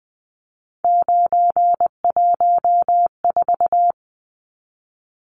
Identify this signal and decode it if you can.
Morse code, heard as a keyed tone, about 20 wpm, 700 Hz. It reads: 914
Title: Numbers - 3-Digit 20wpm